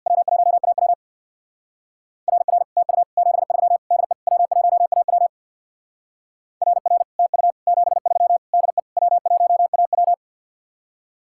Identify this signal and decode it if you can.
Morse code, heard as a keyed tone, about 64 wpm, 700 Hz. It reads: W1AW RR TU 73 DE W1AW RR TU 73 DE W1AW